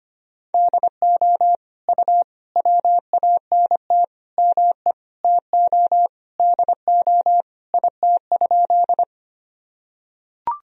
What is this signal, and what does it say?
Morse code, heard as a keyed tone, about 25 wpm, 700 Hz. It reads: DO U WANT ME TO DO IT?
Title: Sentences from Top 200 Words and Abbreviations 25wpm